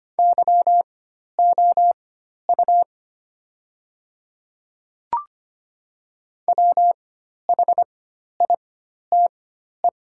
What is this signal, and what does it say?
Morse code, heard as a keyed tone, about 25 wpm, spaced as for 12 wpm, 700 Hz. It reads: YOU WHITE
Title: 4x Character Spacing - Top 300 Words - 25wpm